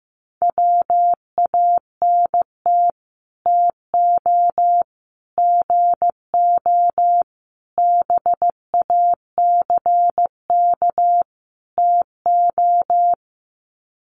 Morse code, heard as a keyed tone, about 15 wpm, 700 Hz. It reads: WANT TO GO BACK TO
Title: Sets of 5 Words 15wpm